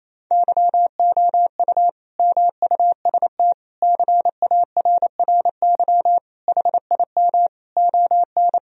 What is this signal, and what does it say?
Morse code, heard as a keyed tone, about 28 wpm, 700 Hz. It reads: YOU MUST CARRY HIM ON
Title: Sentences from Top 1000 Words - Review Part 1 - 28wpm